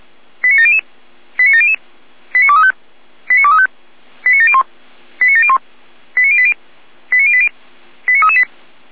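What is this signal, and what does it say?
Five-tone selective call (ZVEI), heard as quick runs of five tones, tones of about 80 ms. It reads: 89700 89700 89235 89235 89701 89701 89080 89080 89338
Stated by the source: ZVEI five-tone selective call 89126 received on SDR.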